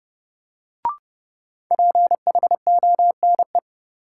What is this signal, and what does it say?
Morse code, heard as a keyed tone, about 30 wpm, 700 Hz. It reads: PHONE
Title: Sentences from Top 1100 Words - New Word Drill - 30wpm